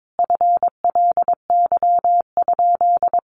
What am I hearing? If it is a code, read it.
Morse code, heard as a keyed tone, about 22 wpm, 700 Hz. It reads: FLY?
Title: Sentences from Top 500 Words 22wpm